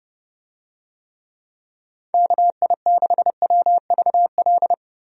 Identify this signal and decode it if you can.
Morse code, heard as a keyed tone, about 30 wpm, 700 Hz. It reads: KI6WVL
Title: USA Call Signs - 6 Characters - Repeated 3x - 30wpm